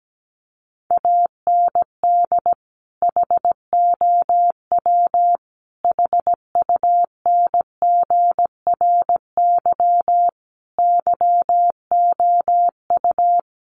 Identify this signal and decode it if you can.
Morse code, heard as a keyed tone, about 17 wpm, 700 Hz. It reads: AND HOW HUNGRY YOU
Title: Sentences from Top 1400 Words 17wpm